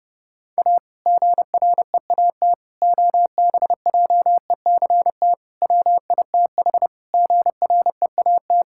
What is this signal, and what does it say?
Morse code, heard as a keyed tone, about 32 wpm, 700 Hz. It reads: A GREAT OBJECT WITH GREAT
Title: Sentences from Top 1600 Words 30wpm